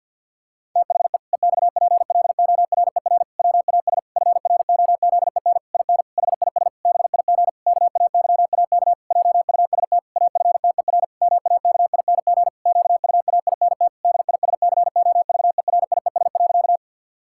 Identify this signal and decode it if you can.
Morse code, heard as a keyed tone, about 50 wpm, 700 Hz. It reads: THE EXPLORER WAS FROZEN IN HIS BIG KAYAK JUST AFTER MAKING QUAINT DISCOVERIES.
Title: Pangram Challenge v2 - 50wpm